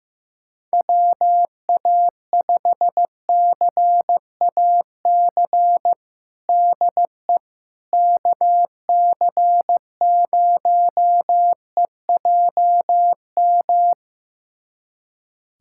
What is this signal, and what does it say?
Morse code, heard as a keyed tone, about 16 wpm, 700 Hz. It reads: WA5CAC DE KC0EJM
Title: USA Callsign DE Callsign 15wpm